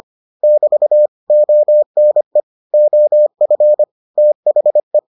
Morse code, heard as a keyed tone, about 25 wpm, 600 Hz. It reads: <BT> ONE OF THE